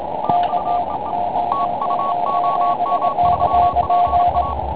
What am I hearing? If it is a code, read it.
Morse code, heard as a keyed tone, about 30 wpm, 1110 Hz, with radio noise. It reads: TU OM3PA
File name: Pile UP CUC204